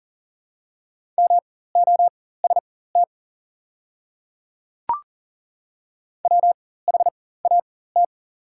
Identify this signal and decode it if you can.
Morse code, heard as a keyed tone, about 40 wpm, spaced as for 19 wpm, 700 Hz. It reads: MOST WHAT
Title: 4x Character Spacing - Top 200 Words - 40wpm